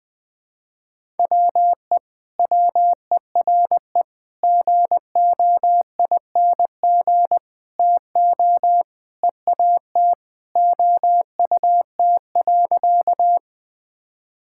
Morse code, heard as a keyed tone, about 20 wpm, 700 Hz. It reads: WE WERE GOING TO EAT OUT.
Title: Sentences with 4-Word Phrases - Spoken Beginning - 20wpm